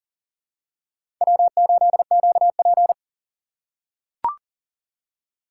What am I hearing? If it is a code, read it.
Morse code, heard as a keyed tone, about 40 wpm, 700 Hz. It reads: W8QP
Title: USA Call Signs - 4x Character Spacing - 40wpm